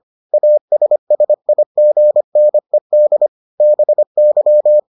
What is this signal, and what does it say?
Morse code, heard as a keyed tone, about 25 wpm, 600 Hz. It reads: ASSIGNED BY